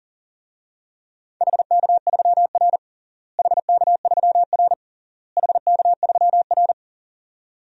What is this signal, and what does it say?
Morse code, heard as a keyed tone, about 40 wpm, 700 Hz. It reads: HK3R HK3R HK3R
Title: DXCC-100 Call Signs - 4 Characters - Repeated 3x - 40wpm